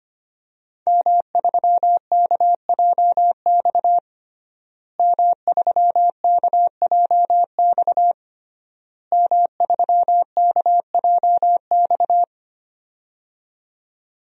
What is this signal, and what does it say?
Morse code, heard as a keyed tone, about 25 wpm, 700 Hz. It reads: M3KJX M3KJX M3KJX
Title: DXCC-100 Call Signs - 5 Characters - Repeated 3x - 25wpm